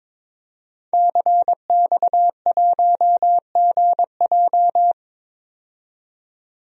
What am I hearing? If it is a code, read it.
Morse code, heard as a keyed tone, about 22 wpm, 700 Hz. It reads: CX1GJ
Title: DXCC-100 Call Signs - 5 Characters - 22wpm